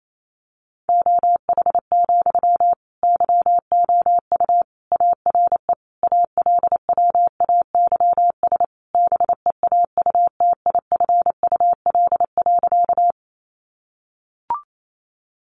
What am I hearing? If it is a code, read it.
Morse code, heard as a keyed tone, about 28 wpm, 690 Hz. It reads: OH, YOU ARE ALWAYS BEAUTIFUL.
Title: Sentences from Top 900 Words 28wpm